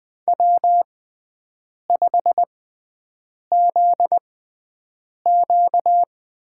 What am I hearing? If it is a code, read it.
Morse code, heard as a keyed tone, about 20 wpm, spaced as for 7 wpm, 700 Hz. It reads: W5ZQ
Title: USA Call Signs - 4 Characters - 6x Character Spacing - 20wpm